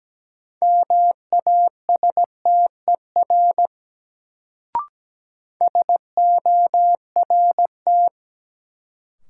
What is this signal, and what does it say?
Morse code, heard as a keyed tone, about 17 wpm, 700 Hz. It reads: MASTER SORT
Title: Sentences from Top 1300 Words - New Word Drill Alt - 17wpm